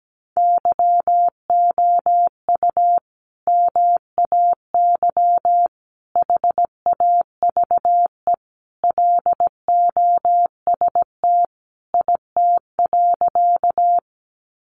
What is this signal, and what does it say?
Morse code, heard as a keyed tone, about 17 wpm, 700 Hz. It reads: YOU MAY HAVE LOST IT.
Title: Sentences from Top 1400 Words 17wpm